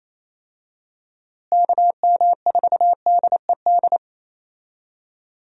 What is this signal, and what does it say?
Morse code, heard as a keyed tone, about 28 wpm, 700 Hz. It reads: KM4DED
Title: USA Call Signs - 6 Characters - Repeated 3x - 28wpm